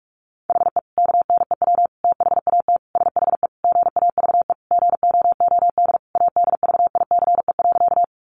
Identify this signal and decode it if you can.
Morse code, heard as a keyed tone, about 44 wpm, 700 Hz. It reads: HE KNEW THAT SHE GAVE GOOD ADVICE.
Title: Sentences from Top 1100 Words 45wpm